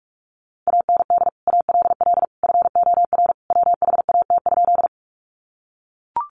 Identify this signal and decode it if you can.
Morse code, heard as a keyed tone, about 44 wpm, 700 Hz. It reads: AND ALL FOR WHAT?
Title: Sentences from Top 100 Words 45wpm